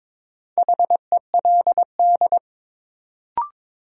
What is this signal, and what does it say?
Morse code, heard as a keyed tone, about 22 wpm, 700 Hz. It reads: HELD
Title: Sentences from Top 1600 Words - New Word Drill - 22wpm